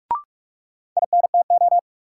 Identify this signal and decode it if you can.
Morse code, heard as a keyed tone, about 42 wpm, 700 Hz. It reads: INTO
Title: Sets of 2 Words - New Word Drill - 45wpm